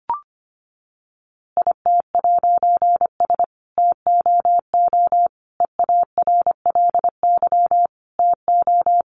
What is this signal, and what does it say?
Morse code, heard as a keyed tone, about 25 wpm, 700 Hz. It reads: IT'S TOO EARLY TO
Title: A Sound Of Thunder 25wpm